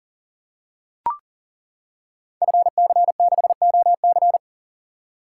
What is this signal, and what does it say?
Morse code, heard as a keyed tone, about 40 wpm, 700 Hz. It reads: FC6OC
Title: Code Groups - 5 Letter:Numbers - 40wpm